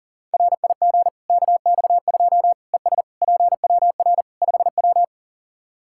Morse code, heard as a keyed tone, about 40 wpm, 700 Hz. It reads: RIG KX2 ES PWR 5W
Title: K1USN SST QSO Practice 40wpm